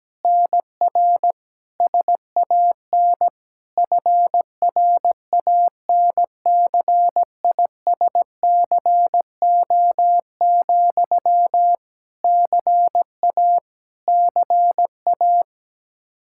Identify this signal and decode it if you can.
Morse code, heard as a keyed tone, about 17 wpm, 700 Hz. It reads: NR SAN FRANCISCO, CA CA
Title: Practice QSO - Exchange 1 - Beginning - 17wpm